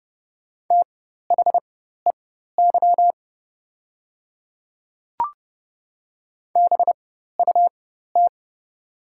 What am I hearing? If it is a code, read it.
Morse code, heard as a keyed tone, about 30 wpm, spaced as for 14 wpm, 690 Hz. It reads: THEY BUT
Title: 4x Character Spacing - Top 100 Words - 30wpm